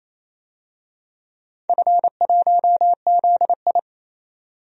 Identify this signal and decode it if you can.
Morse code, heard as a keyed tone, about 28 wpm, 700 Hz. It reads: F1ZI
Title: DXCC-100 Call Signs - 4 Characters - Repeated 3x - 28wpm